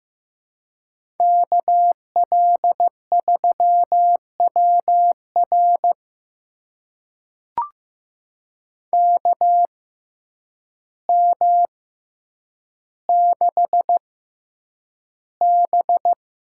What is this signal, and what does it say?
Morse code, heard as a keyed tone, about 15 wpm, spaced as for 8 wpm, 700 Hz. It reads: KL3WR KM6B
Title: USA Call Signs - 6x Character Spacing - 15wpm